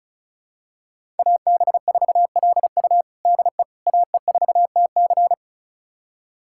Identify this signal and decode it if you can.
Morse code, heard as a keyed tone, about 36 wpm, 700 Hz. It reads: AB4LU DE AE4TC
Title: USA Callsign DE Callsign - 5 Characters - 35wpm